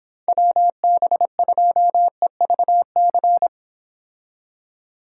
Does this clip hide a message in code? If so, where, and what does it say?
Yes: Morse code from 0.3 to 3.5 s, heard as a keyed tone, about 30 wpm, 700 Hz. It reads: WB2EVC